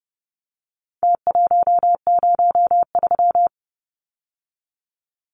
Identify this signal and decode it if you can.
Morse code, heard as a keyed tone, about 30 wpm, 700 Hz. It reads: T103
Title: Numbers - Contest Style - 4-Digit 30wpm